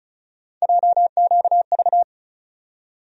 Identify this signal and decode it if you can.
Morse code, heard as a keyed tone, about 36 wpm, 700 Hz. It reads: JQV
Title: USA Call Sign Postfixes - 3 Letters 35wpm